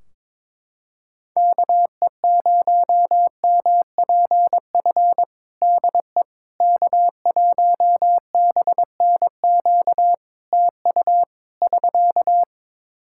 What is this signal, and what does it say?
Morse code, heard as a keyed tone, about 22 wpm, 700 Hz. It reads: KE0MPF DE K1BNQ TU <SK>
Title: Practice QSO - Exchange 4 - Ending - 22wpm